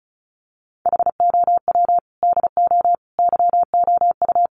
Morse code, heard as a keyed tone, about 34 wpm, 700 Hz. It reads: HOW DO YOU
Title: Sentences from Top 1100 Words 35wpm